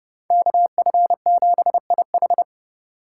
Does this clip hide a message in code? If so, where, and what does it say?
Yes: Morse code from 0.3 to 2.4 s, heard as a keyed tone, about 39 wpm, 700 Hz. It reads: KF7IH